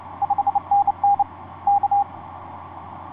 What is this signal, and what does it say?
Morse code, heard as a keyed tone, about 29 wpm, 810 Hz. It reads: 5NN K
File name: ys1zc ik2cik etc 30m cw 07feb2010 2249